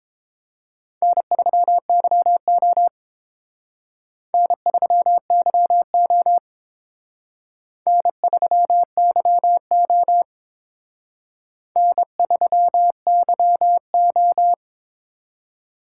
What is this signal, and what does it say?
Morse code, heard as a keyed tone, about 27 wpm, 700 Hz. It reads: N3YO N3YO N3YO N3YO
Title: Speed Racing - USA 1x2 Call Signs 22wpm